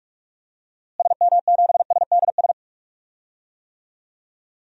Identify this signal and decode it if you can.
Morse code, heard as a keyed tone, about 46 wpm, 700 Hz. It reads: SM7SDS
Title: DXCC-100 Call Signs - 6 Characters - Repeated 3x - 45wpm